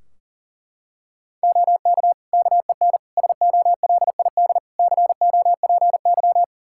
Morse code, heard as a keyed tone, about 40 wpm, 700 Hz. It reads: OK KEN SOLID COPY
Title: Practice QSO - Exchange 3 - More Info - 40wpm